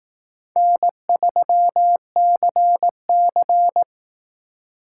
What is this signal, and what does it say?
Morse code, heard as a keyed tone, about 18 wpm, 700 Hz. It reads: N3CC
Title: Speed Racing - USA Call Signs 15wpm